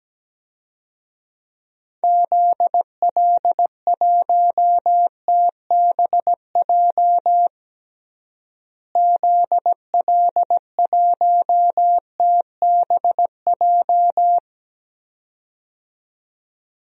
Morse code, heard as a keyed tone, about 17 wpm, 700 Hz. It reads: ZL1TBJ ZL1TBJ
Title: DXCC-100 Call Signs - 6 Characters - Repeated 2x - 17wpm